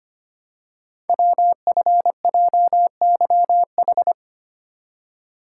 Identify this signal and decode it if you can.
Morse code, heard as a keyed tone, about 25 wpm, 700 Hz. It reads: WFJYH
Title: Code Groups - 5 Letters - 25wpm